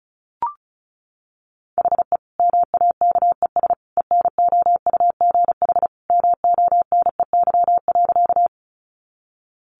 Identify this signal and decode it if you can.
Morse code, heard as a keyed tone, about 34 wpm, 700 Hz. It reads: HE MAKES ENOUGH MONEY.
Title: Additional Easy Phrases 35wpm